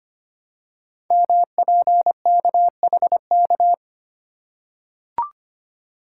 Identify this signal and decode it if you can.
Morse code, heard as a keyed tone, about 25 wpm, 700 Hz. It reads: MPKHK
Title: Code Groups - 5 Letter:Numbers - 25wpm